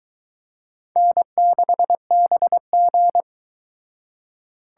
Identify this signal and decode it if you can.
Morse code, heard as a keyed tone, about 23 wpm, 700 Hz. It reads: N6BG